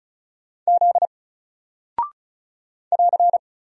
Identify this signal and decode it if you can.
Morse code, heard as a keyed tone, about 33 wpm, 700 Hz. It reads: Z <AR>
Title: Single Character (Letter:Number:Punctuation:Prosign) 35wpm